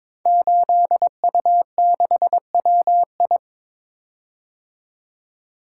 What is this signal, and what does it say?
Morse code, heard as a keyed tone, about 22 wpm, 700 Hz. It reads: 8U6WI
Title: Code Groups - 5 Letter:Numbers - 22wpm